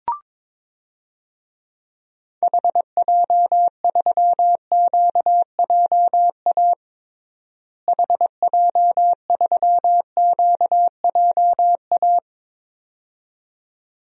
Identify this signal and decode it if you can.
Morse code, heard as a keyed tone, about 22 wpm, 700 Hz. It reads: HJ3QJA HJ3QJA
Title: DXCC-100 Call Signs - 6 Characters - Repeated 2x - 22wpm